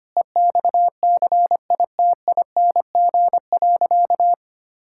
Morse code, heard as a keyed tone, about 25 wpm, 700 Hz. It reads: EXCITING.
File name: practice / 25wpm / Sentences with 4-Word Phrases - 25wpm